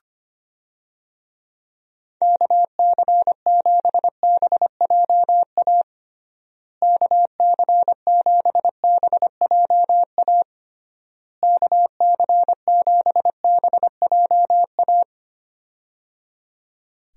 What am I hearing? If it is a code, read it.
Morse code, heard as a keyed tone, about 25 wpm, 700 Hz. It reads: KC7BJA KC7BJA KC7BJA
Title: USA Call Signs - 6 Characters - Repeated 3x - 25wpm